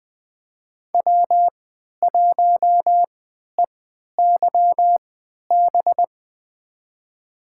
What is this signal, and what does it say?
Morse code, heard as a keyed tone, about 21 wpm, spaced as for 12 wpm, 700 Hz. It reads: W1EYB
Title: USA Call Signs - 5 Characters - 3x Character Spacing - 20wpm